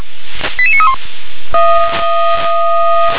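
Five-tone selective call (ZVEI), heard as quick runs of five tones, tones of about 70 ms. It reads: 88931
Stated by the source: ZVEI five-tone selective call received by SDR.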